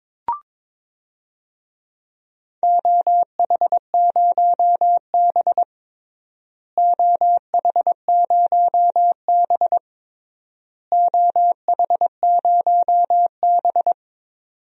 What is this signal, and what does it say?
Morse code, heard as a keyed tone, about 22 wpm, 700 Hz. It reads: OH0B OH0B OH0B
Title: DXCC-100 Call Signs - 4 Characters - Repeated 3x - 22wpm